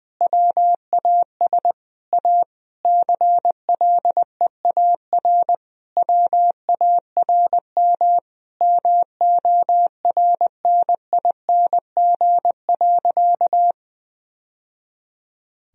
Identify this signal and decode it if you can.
Morse code, heard as a keyed tone, about 20 wpm, 700 Hz. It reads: WAS A CLEAR WARM MORNING.
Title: Sentences from Top 1000 Words - Review Part 2 - 20wpm